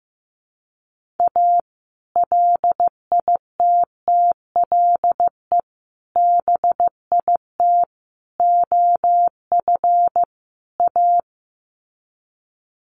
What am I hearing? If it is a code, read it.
Morse code, heard as a keyed tone, about 15 wpm, 700 Hz. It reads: A LITTLE BIT OF A
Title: Sets of 5 Words 15wpm